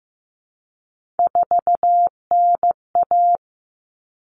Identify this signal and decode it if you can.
Morse code, heard as a keyed tone, about 15 wpm, 700 Hz. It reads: 4NA
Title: Common Cut Numbers - 3-Digit 15wpm